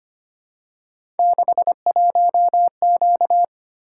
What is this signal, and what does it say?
Morse code, heard as a keyed tone, about 25 wpm, 700 Hz. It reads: 61Q
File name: Three Letter-Number Combinations - 25wpm